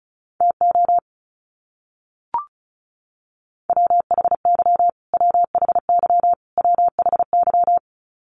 Morse code, heard as a keyed tone, about 34 wpm, 700 Hz. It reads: TO WHY WHY WHY
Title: Top 500 Words Repeated 35wpm